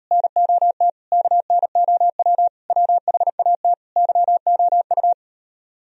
Morse code, heard as a keyed tone, about 38 wpm, 700 Hz. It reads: NOT KNOW WHAT YOU